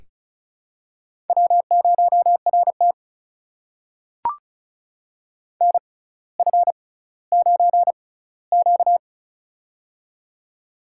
Morse code, heard as a keyed tone, about 36 wpm, spaced as for 18 wpm, 700 Hz. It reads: W0RT NF9Q
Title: USA Call Signs - 4 Characters - 6x Character Spacing - 35wpm